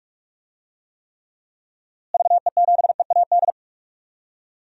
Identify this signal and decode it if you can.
Morse code, heard as a keyed tone, about 41 wpm, 700 Hz. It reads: VE7EAD
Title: DXCC-100 Call Signs - 6 Characters - Repeated 3x - 45wpm